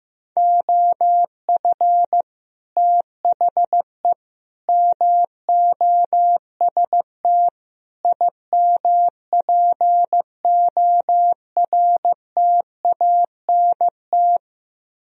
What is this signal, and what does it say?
Morse code, heard as a keyed tone, about 15 wpm, 700 Hz. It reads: OF THE MOST IMPORTANT
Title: Speed-Racing - Sets of 4 Words 15wpm